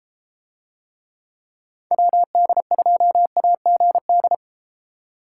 Morse code, heard as a keyed tone, about 32 wpm, 700 Hz. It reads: WD2AGD